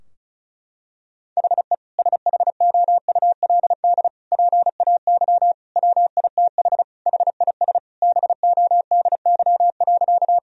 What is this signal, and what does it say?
Morse code, heard as a keyed tone, about 34 wpm, 700 Hz. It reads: HE SHOULD PAY WITH HIS BODY.
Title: Sentences from Top 1000 Words - Review Part 2 - 35wpm